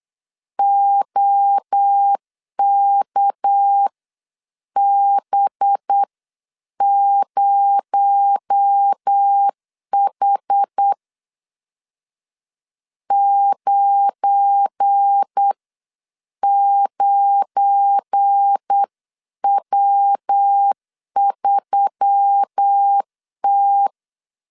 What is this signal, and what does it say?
Morse code, heard as a keyed tone, about 8 wpm, spaced as for 7 wpm, 800 Hz. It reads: OKB0H 99W3T
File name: LEZIONI / 7 GF9Z / 7 GF9 8wpm 3545